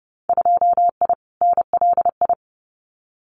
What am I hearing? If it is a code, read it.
Morse code, heard as a keyed tone, about 30 wpm, 700 Hz. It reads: 2I NLI
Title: Winter Field Day Exchanges 30wpm